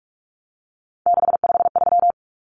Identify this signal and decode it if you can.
Morse code, heard as a keyed tone, about 44 wpm, 700 Hz. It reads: 653
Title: Numbers - 3-Digit 45wpm